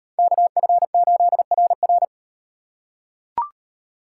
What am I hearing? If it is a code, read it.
Morse code, heard as a keyed tone, about 38 wpm, 700 Hz. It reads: KF8RR